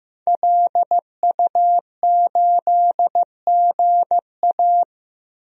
Morse code, heard as a keyed tone, about 16 wpm, 700 Hz. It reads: LU8GA
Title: DXCC-100 Call Signs - 5 Characters - Repeated 2x - 15wpm